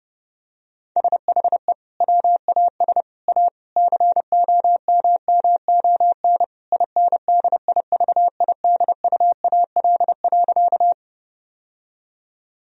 Morse code, heard as a keyed tone, about 30 wpm, 700 Hz. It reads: SHE WAS A COMMON INDIVIDUAL.
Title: Sentences from Top 1000 Words - Review Part 4 - 30wpm